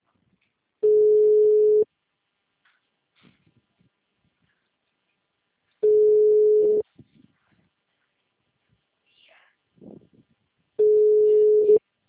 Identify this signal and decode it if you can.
Ringback tone (Europe, 425 Hz).
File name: out- -2026- - - .67